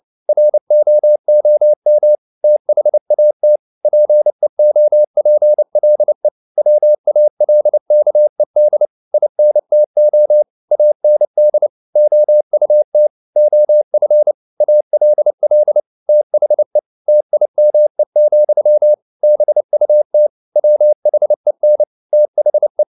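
Morse code, heard as a keyed tone, about 29 wpm, 600 Hz. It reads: ROOM THAT PEOPLE WALKED INTO AND OUT OF ALL THE TIME, BUT WHEN THE
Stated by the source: Morse code at 29 wpm, sending Ken Follett's novel Never in English, from CW med Gnister.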